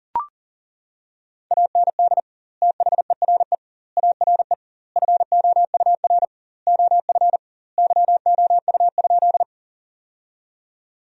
Morse code, heard as a keyed tone, about 40 wpm, 700 Hz. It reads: AND THERE ARE FOUR OF YOU?
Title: 2x Word-Spacing - Sentences from Top 300 Words - 40wpm